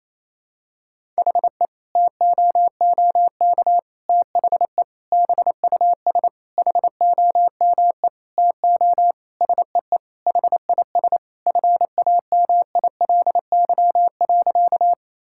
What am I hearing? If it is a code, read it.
Morse code, heard as a keyed tone, about 28 wpm, 700 Hz. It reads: HE TOOK THE BUS HOME TO SEE HIS FAMILY.
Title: Sentences from Top 1000 Words - Review Part 6 - 28wpm